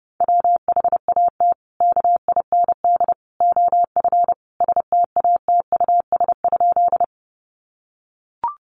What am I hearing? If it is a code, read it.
Morse code, heard as a keyed tone, about 30 wpm, 700 Hz. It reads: WHAT KIND OF STATUS?
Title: Sentences from Top 1400 Words 30wpm